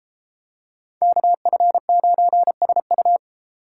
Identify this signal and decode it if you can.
Morse code, heard as a keyed tone, about 33 wpm, 700 Hz. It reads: KF9SU